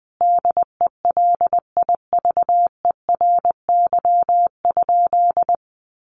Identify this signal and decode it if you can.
Morse code, heard as a keyed tone, about 19 wpm, 700 Hz. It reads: DELIVERY?
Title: Sentences from Top 1500 Words 20wpm